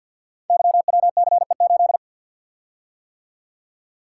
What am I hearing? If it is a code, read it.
Morse code, heard as a keyed tone, about 52 wpm, 700 Hz. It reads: YWCE8
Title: Code Groups - 5 Letter:Numbers - 50wpm